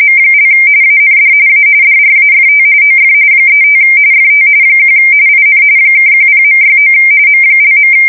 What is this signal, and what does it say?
RTTY, about 45.45 baud. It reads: M) AND A DRAFT OF 35 FT 1 IN (10.69 M).